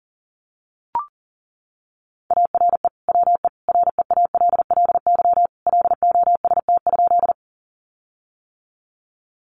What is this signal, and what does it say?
Morse code, heard as a keyed tone, about 40 wpm, 700 Hz. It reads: ARE WE REALLY LOST?
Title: Sentences from Top 1400 Words 40wpm